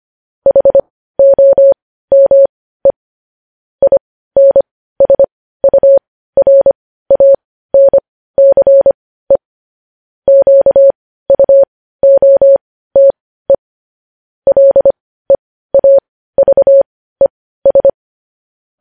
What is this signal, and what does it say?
Morse code, heard as a keyed tone, about 25 wpm, spaced as for 15 wpm, 550 Hz. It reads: HOME INSURANCE QUOTE LEAVES
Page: news headlines in morse code at 15 wpm